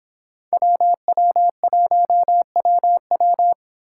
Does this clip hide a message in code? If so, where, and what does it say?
Yes: Morse code from 0.5 to 3.5 s, heard as a keyed tone, about 26 wpm, 700 Hz. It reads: WW1WW